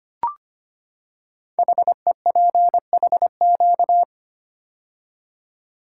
Morse code, heard as a keyed tone, about 25 wpm, 700 Hz. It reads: HEPHQ
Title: Code Groups - 5 Letters - 25wpm